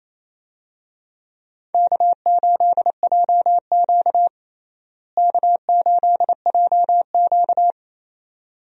Morse code, heard as a keyed tone, about 28 wpm, 700 Hz. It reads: K8JQ K8JQ
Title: USA Call Signs - 4 Characters - Repeated 2x - 28wpm